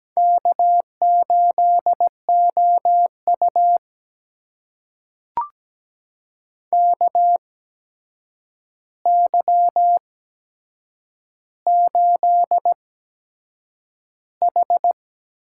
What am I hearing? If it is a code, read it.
Morse code, heard as a keyed tone, about 17 wpm, spaced as for 7 wpm, 700 Hz. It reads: K8OU KY8H
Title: USA Call Signs - 4 Characters - 8x Character Spacing - 17wpm